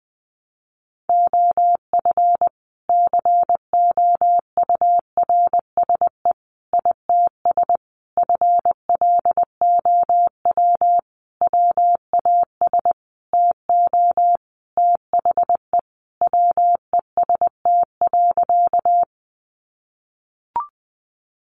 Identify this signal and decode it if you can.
Morse code, heard as a keyed tone, about 20 wpm, 700 Hz. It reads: OF COURSE ITS FLOW WAS TO THE WEST.
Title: Sentences from Top 1200 Words 20wpm